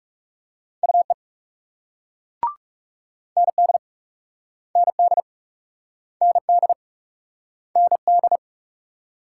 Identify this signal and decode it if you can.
Morse code, heard as a keyed tone, about 42 wpm, 700 Hz. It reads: UE ND ND ND ND